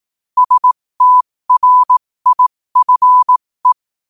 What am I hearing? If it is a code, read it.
Morse code, heard as a keyed tone, about 18 wpm, spaced as for 15 wpm, 1000 Hz. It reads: STRIFE
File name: qotdfiles / 15wpm